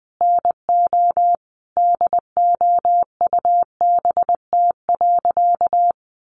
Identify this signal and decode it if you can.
Morse code, heard as a keyed tone, about 20 wpm, 700 Hz. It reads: NO DOUBT.
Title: Sentences from Top 1500 Words 20wpm